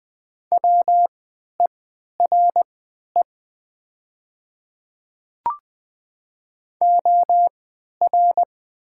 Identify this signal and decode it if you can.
Morse code, heard as a keyed tone, about 20 wpm, spaced as for 11 wpm, 700 Hz. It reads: WERE OR
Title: 3x Character Spacing - Top 100 Words - 20wpm